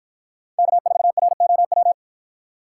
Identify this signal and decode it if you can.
Morse code, heard as a keyed tone, about 50 wpm, 700 Hz. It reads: K4ROW